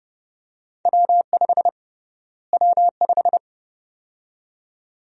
Morse code, heard as a keyed tone, about 30 wpm, 700 Hz. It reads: W5 W5
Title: USA Call Sign Prefixes - Repeated 2x - 30wpm